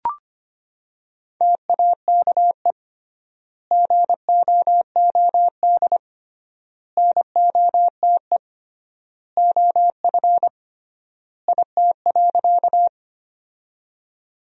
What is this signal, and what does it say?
Morse code, heard as a keyed tone, about 25 wpm, 700 Hz. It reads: TAKE GOOD NOTE OF IT.
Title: 3x Word-Spacing - Sentences from Top 500 Words - 25wpm